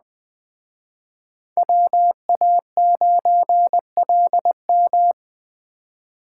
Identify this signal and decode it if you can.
Morse code, heard as a keyed tone, about 20 wpm, 700 Hz. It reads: WA9LM
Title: Speed Racing - USA Call Signs 17wpm